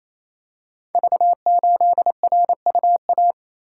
Morse code, heard as a keyed tone, about 27 wpm, 700 Hz. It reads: V8RUA
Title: Code Groups - 5 Letter:Numbers - 28wpm